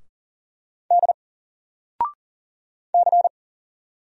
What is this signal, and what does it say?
Morse code, heard as a keyed tone, about 40 wpm, 700 Hz. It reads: D C
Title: Single Letter-Number Alt 40wpm